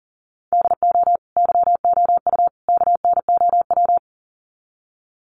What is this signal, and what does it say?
Morse code, heard as a keyed tone, about 40 wpm, 700 Hz. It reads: DO YOU KNOW
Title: Sets of 3 Words 40wpm